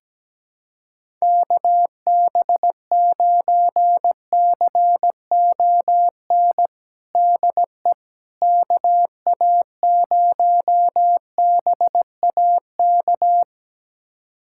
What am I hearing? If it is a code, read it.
Morse code, heard as a keyed tone, about 17 wpm, 700 Hz. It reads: KB9CON DE KA0BAK
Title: USA Callsign DE Callsign 17wpm